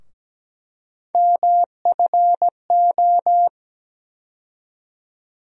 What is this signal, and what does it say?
Morse code, heard as a keyed tone, about 17 wpm, 700 Hz. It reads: MFO
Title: USA Call Sign Postfixes - 3 Letters 17wpm